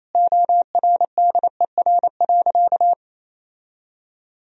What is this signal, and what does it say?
Morse code, heard as a keyed tone, about 28 wpm, 700 Hz. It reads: ORDER.
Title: Sentences from Top 1000 Words - Review Part 2 - 28wpm